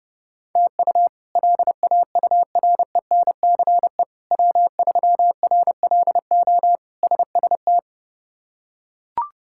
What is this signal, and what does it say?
Morse code, heard as a keyed tone, about 30 wpm, 700 Hz. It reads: TU LAURENCE W3RLO SST
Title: K1USN SST QSO Practice - 30wpm